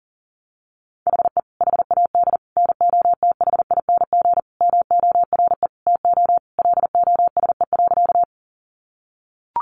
Morse code, heard as a keyed tone, about 40 wpm, 700 Hz. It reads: HE HAD NOTHING MORE TO LOSE.
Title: Sentences from Top 1100 Words 40wpm